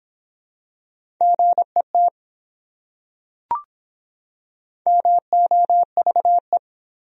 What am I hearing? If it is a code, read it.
Morse code, heard as a keyed tone, about 24 wpm, 700 Hz. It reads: GET MOVE